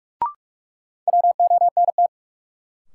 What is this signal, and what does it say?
Morse code, heard as a keyed tone, about 46 wpm, 700 Hz. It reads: WONT
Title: Sentences from Top 1000 Words - Review Part 6 - New Word Drill - 45wpm